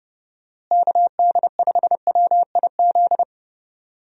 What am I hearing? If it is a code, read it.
Morse code, heard as a keyed tone, about 30 wpm, 700 Hz. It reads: KD5WIZ
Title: USA Call Signs - 6 Characters - Repeated 2x - 30wpm